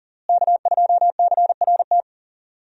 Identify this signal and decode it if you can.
Morse code, heard as a keyed tone, about 40 wpm, 700 Hz. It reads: K2CRT
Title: USA Call Signs - 5 Characters - Repeated 3x - 40wpm